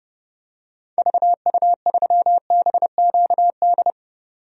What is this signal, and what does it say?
Morse code, heard as a keyed tone, about 32 wpm, 700 Hz. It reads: VU3BQD
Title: DXCC-100 Call Signs - 6 Characters - 30wpm